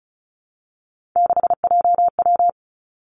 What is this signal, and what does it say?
Morse code, heard as a keyed tone, about 36 wpm, 700 Hz. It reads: 6JW
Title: Three Challenging Letter-Number Combinations 35wpm